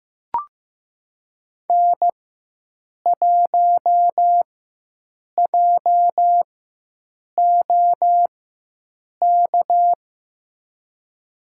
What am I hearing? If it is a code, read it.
Morse code, heard as a keyed tone, about 15 wpm, spaced as for 7 wpm, 700 Hz. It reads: N1JOK
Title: USA Call Signs - 5 Characters - 4x Character Spacing - 15wpm